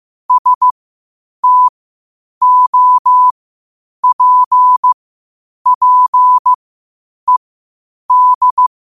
Morse code, heard as a keyed tone, about 15 wpm, spaced as for 8 wpm, 1000 Hz. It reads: STOPPED
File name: qotdfiles / 8wpm